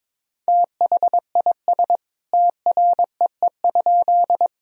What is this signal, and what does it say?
Morse code, heard as a keyed tone, about 21 wpm, 700 Hz. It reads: THIS TREE?
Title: Sentences from Top 1000 Words - Review Part 1 - 22wpm